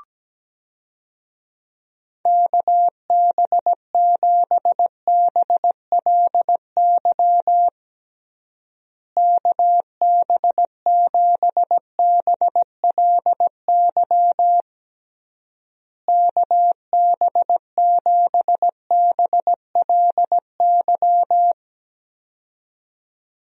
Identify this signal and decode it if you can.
Morse code, heard as a keyed tone, about 17 wpm, 700 Hz. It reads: KB7BLY KB7BLY KB7BLY
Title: USA Call Signs - 6 Characters - Repeated 3x - 17wpm